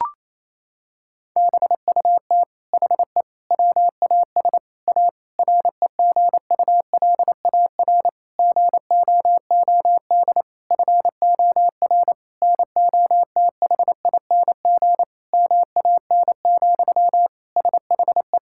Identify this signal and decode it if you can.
Morse code, heard as a keyed tone, about 28 wpm, 700 Hz. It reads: BUT HE WAS A REGULAR GOOD FOR NOTHING MAN, SHE
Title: Sentences from Top 1000 Words - Review Part 6 - 28wpm